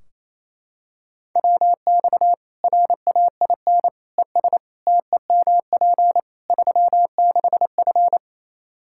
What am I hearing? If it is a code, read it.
Morse code, heard as a keyed tone, about 28 wpm, 700 Hz. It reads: WX RAIN ES TEMP 36F
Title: Practice QSO - Exchange 2 - Info - 28wpm